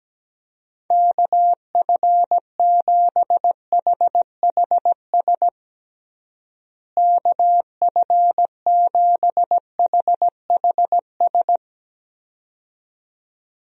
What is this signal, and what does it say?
Morse code, heard as a keyed tone, about 17 wpm, 700 Hz. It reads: KF7HHS KF7HHS
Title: USA Call Signs - 6 Characters - Repeated 2x - 17wpm